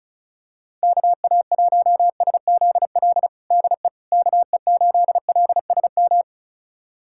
Morse code, heard as a keyed tone, about 36 wpm, 700 Hz. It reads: KA1SZL DE KE8LSM
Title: USA Callsign DE Callsign - 6 Characters - 35wpm